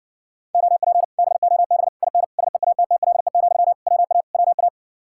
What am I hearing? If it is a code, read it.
Morse code, heard as a keyed tone, about 51 wpm, 700 Hz. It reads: OP BOB IN SEATTLE, WA WA